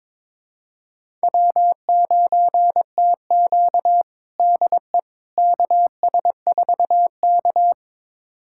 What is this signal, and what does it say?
Morse code, heard as a keyed tone, about 22 wpm, 700 Hz. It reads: W9TQ DE KS4K
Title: USA Callsign DE Callsign - 4 Characters - 22wpm